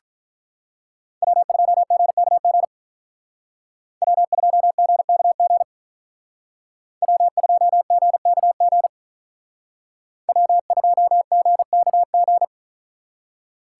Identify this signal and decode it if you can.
Morse code, heard as a keyed tone, about 43 wpm, 700 Hz. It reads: W2GKG W2GKG W2GKG W2GKG
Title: Speed Racing - USA 1x3 Call Signs 35wpm